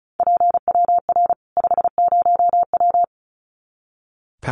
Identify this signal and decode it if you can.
Morse code, heard as a keyed tone, about 34 wpm, 700 Hz. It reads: PWR 50W
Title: Common QSO Phrases 35wpm